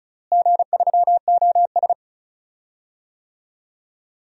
Morse code, heard as a keyed tone, about 36 wpm, 690 Hz. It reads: G3OS
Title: DXCC-100 Call Signs - 4 Characters - Repeated 2x - 35wpm